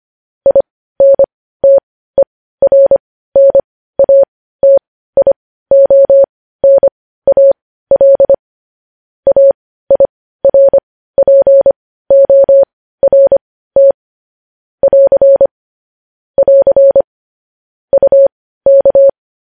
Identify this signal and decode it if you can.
Morse code, heard as a keyed tone, about 25 wpm, spaced as for 15 wpm, 550 Hz. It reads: INTERNATIONAL AIRPORT <AR> <AR> UK
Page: News Headlines in Morse Code at 15 WPM